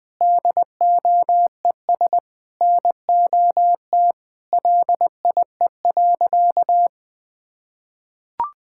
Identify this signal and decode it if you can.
Morse code, heard as a keyed tone, about 20 wpm, 700 Hz. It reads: DOES NOT LIE.
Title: Sentences from Top 1600 Words 20wpm